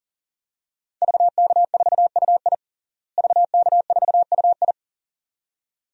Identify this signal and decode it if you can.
Morse code, heard as a keyed tone, about 40 wpm, 700 Hz. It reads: VK4UI VK4UI
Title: DXCC-100 Call Signs - 5 Characters - Repeated 2x - 40wpm